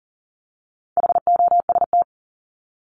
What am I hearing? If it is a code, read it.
Morse code, heard as a keyed tone, about 40 wpm, 700 Hz. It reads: HOST
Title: Top 1501-1750 Words 40wpm